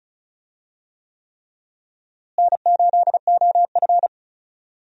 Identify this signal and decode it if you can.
Morse code, heard as a keyed tone, about 36 wpm, 700 Hz. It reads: N8OF
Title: USA Call Signs - 4 Characters - Repeated 3x - 35wpm